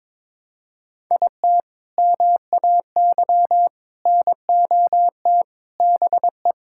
Morse code, heard as a keyed tone, about 22 wpm, 700 Hz. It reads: IT MAY NOT BE
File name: practice / 22wpm / Sentences with 3-Word Phrases - 22wpm